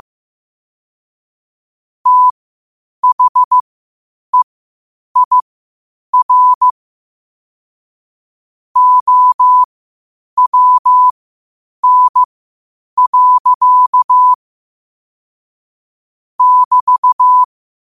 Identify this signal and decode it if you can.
Morse code, heard as a keyed tone, about 15 wpm, spaced as for 8 wpm, 1000 Hz. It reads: THEIR OWN. <BT>
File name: qotdfiles / 8wpm